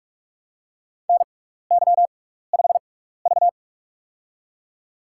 Morse code, heard as a keyed tone, about 46 wpm, spaced as for 15 wpm, 700 Hz. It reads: NY5V
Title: USA Call Signs - 4 Characters - 6x Character Spacing - 45wpm